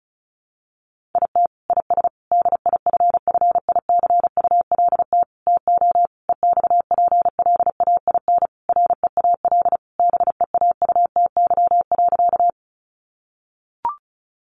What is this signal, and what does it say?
Morse code, heard as a keyed tone, about 34 wpm, 700 Hz. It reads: IT IS DIFFICULT TO EXPLAIN REAL BEAUTY.
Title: Sentences from Top 1100 Words 35wpm